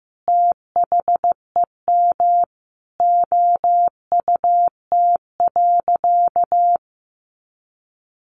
Morse code, heard as a keyed tone, about 15 wpm, 700 Hz. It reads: THEM OUT.
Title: Sentences from Top 500 Words 15wpm